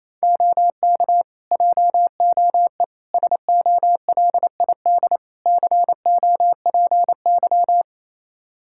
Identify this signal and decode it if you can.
Morse code, heard as a keyed tone, about 28 wpm, 700 Hz. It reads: OK JOE SOLID COPY
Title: Practice QSO - Exchange 3 - More Info - 28wpm